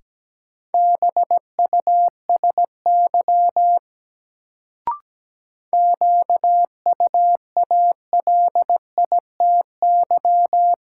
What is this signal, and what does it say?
Morse code, heard as a keyed tone, about 17 wpm, 700 Hz. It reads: BUSY QUALITY
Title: Sentences from Top 1500 Words - New Word Drill Alt - 17wpm